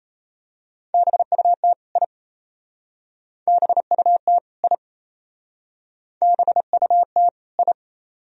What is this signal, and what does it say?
Morse code, heard as a keyed tone, about 32 wpm, 700 Hz. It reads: BUT I BUT I BUT I
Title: Speed-Racing - Sets of 2 Words 28wpm